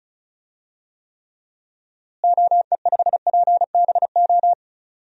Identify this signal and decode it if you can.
Morse code, heard as a keyed tone, about 36 wpm, 700 Hz. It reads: OE5PBO
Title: DXCC-100 Call Signs - 6 Characters - Repeated 2x - 35wpm